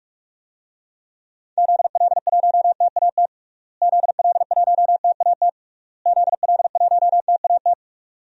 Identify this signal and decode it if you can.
Morse code, heard as a keyed tone, about 41 wpm, 700 Hz. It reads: ZL1TAT ZL1TAT ZL1TAT
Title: DXCC-100 Call Signs - 6 Characters - Repeated 3x - 45wpm